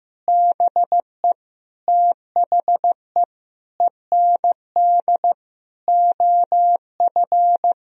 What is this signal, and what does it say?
Morse code, heard as a keyed tone, about 15 wpm, 700 Hz. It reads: BE THE END OF
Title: Sentences from Top 1000 Words - Review Part 3 - 15wpm